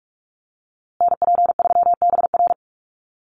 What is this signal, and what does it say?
Morse code, heard as a keyed tone, about 44 wpm, 700 Hz. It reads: NP3BR
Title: USA Call Signs Repeated 2x 45wpm